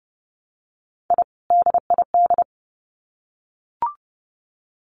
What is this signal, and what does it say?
Morse code, heard as a keyed tone, about 29 wpm, 700 Hz. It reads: I DID
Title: Sets of 2 Words 30wpm